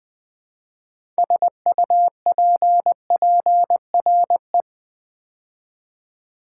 Morse code, heard as a keyed tone, about 20 wpm, 700 Hz. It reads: SUPPRE
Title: Build A Word - 2 Letters At A Time - 8-Letter Words - 20wpm